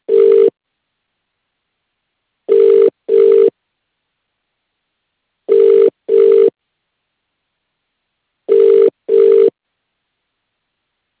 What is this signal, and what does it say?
Ringback tone (UK), 400 + 450 Hz.